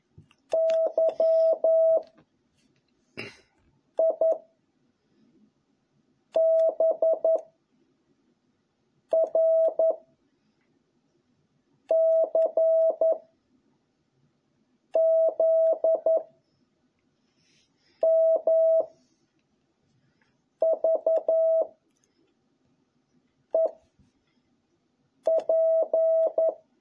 Morse code, heard as a keyed tone, about 11 wpm, 660 Hz. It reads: Y I B R C Z M V E P